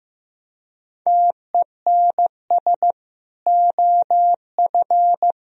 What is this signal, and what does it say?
Morse code, heard as a keyed tone, about 15 wpm, 700 Hz. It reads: TENS OF
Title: Speed-Racing - Sets of 4 Words 15wpm